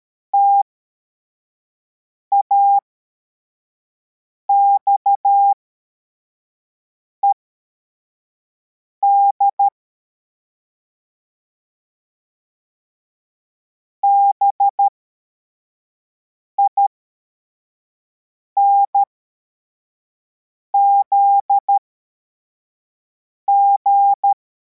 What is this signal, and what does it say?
Morse code, heard as a keyed tone, about 13 wpm, spaced as for 4 wpm, 800 Hz. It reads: TAXED BINZG